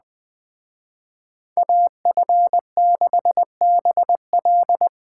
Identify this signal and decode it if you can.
Morse code, heard as a keyed tone, about 21 wpm, 700 Hz. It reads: AF6BL
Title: Speed Racing - USA 2x2 Call Signs 17wpm